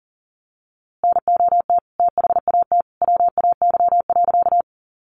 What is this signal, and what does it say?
Morse code, heard as a keyed tone, about 37 wpm, 700 Hz. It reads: NOT THAT WAY.
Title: A Sound Of Thunder 40wpm